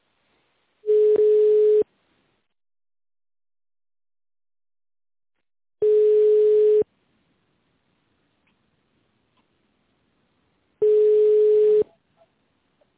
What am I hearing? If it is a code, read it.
Ringback tone (Europe, 425 Hz).